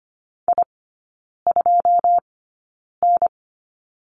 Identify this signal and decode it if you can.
Morse code, heard as a keyed tone, about 25 wpm, spaced as for 9 wpm, 690 Hz. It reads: I2N
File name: Lesson 14e - Number 2 - Three Characters 25wpm